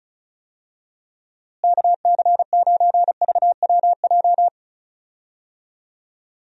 Morse code, heard as a keyed tone, about 34 wpm, 700 Hz. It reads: KC9VWJ
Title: USA Call Signs - 6 Characters - Repeated 3x - 35wpm